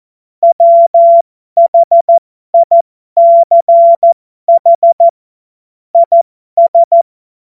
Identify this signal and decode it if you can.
Morse code, heard as a keyed tone, about 14 wpm, spaced as for 12 wpm, 680 Hz. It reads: WHICH IS
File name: Morsekurs / Witze